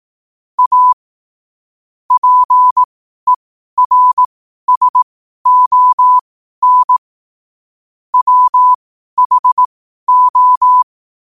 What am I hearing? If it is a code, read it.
Morse code, heard as a keyed tone, about 18 wpm, spaced as for 12 wpm, 1000 Hz. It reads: A PERSON WHO
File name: qotdfiles / 12wpm